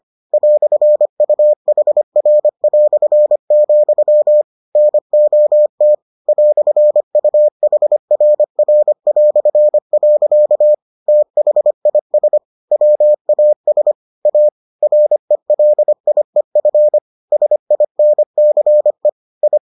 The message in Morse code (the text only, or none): "UHR", NOT "UHRR". THIS WAS A RELIEF SINCE I